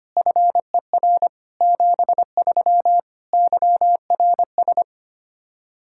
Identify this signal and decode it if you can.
Morse code, heard as a keyed tone, about 25 wpm, 700 Hz. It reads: FER 73 YRS
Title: Practice QSO - Exchange 3 - More Info - 25wpm